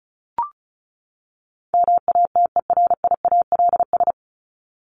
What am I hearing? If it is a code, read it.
Morse code, heard as a keyed tone, about 34 wpm, 700 Hz. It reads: MATERIALS
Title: ES:S Suffix - Top 100 Words with Base Words 35wpm